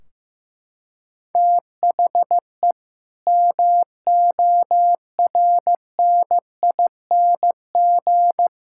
Morse code, heard as a keyed tone, about 15 wpm, 700 Hz. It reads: THE MORNING
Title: Sentences from Top 1600 Words 15wpm